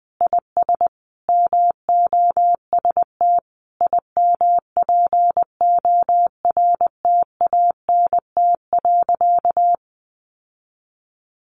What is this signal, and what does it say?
Morse code, heard as a keyed tone, about 19 wpm, 700 Hz. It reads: IS MOST IMPORTANT.
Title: Sentences from Top 1200 Words 20wpm